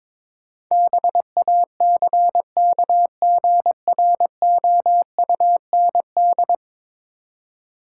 Morse code, heard as a keyed tone, about 22 wpm, 700 Hz. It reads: BACKGROUND
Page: morse-code-ninja-practice-22wpm